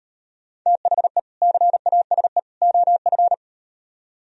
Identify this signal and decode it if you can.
Morse code, heard as a keyed tone, about 38 wpm, 700 Hz. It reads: THE CASE OF